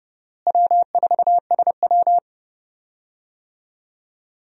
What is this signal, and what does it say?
Morse code, heard as a keyed tone, about 32 wpm, 700 Hz. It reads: W4SW
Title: USA Call Signs - 4 Characters - Repeated 3x - 30wpm